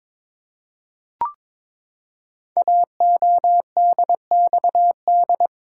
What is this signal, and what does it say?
Morse code, heard as a keyed tone, about 22 wpm, 700 Hz. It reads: AODXD
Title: Code Groups - 5 Letters - 22wpm